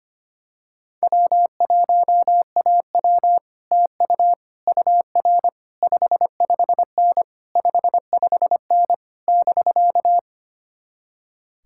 Morse code, heard as a keyed tone, about 24 wpm, 700 Hz. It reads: W1AW TU UR 55N 55N <BK>
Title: Speed-Racing Common QSO Phrases - 25wpm